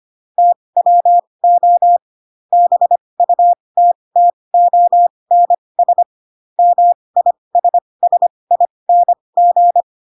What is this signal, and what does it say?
Morse code, heard as a keyed tone, about 25 wpm, spaced as for 20 wpm, 700 Hz. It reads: TWO BUTTONS MISSING
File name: Pinocchio / Pinocchio ch2 15 25 ik1hge